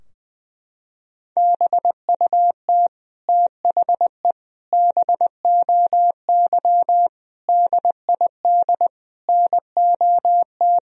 Morse code, heard as a keyed tone, about 21 wpm, 700 Hz. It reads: BUT THE BOY DID NOT
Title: Sentences from Top 1000 Words - Review Part 3 - 20wpm